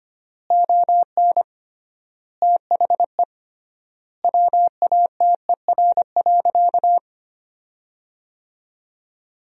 Morse code, heard as a keyed tone, about 25 wpm, 700 Hz. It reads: ON THE WATER.
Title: 3x Word-Spacing - Sentences from Top 500 Words - 25wpm